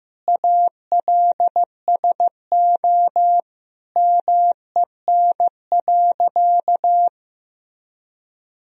Morse code, heard as a keyed tone, about 15 wpm, 700 Hz. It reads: ALSO MEN.
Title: Sentences from Top 1000 Words - Review Part 1 - 15wpm